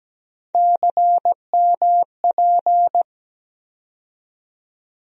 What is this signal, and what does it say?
Morse code, heard as a keyed tone, about 17 wpm, 700 Hz. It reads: CMP
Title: USA Call Sign Postfixes - 3 Letters 17wpm